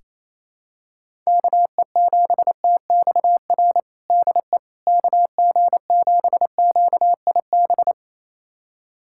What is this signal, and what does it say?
Morse code, heard as a keyed tone, about 27 wpm, 700 Hz. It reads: KE7TXR DE KG7QIB
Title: USA Callsign DE Callsign - 6 Characters - 28wpm